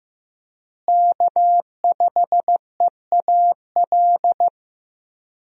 Morse code, heard as a keyed tone, about 15 wpm, 700 Hz. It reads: K5EAL
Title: USA Call Signs - 5 Characters - Repeated 3x - 15wpm